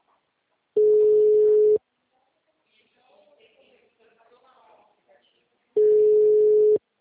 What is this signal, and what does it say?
Ringback tone (Europe, 425 Hz), with speech.
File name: out- -2001- - - .29512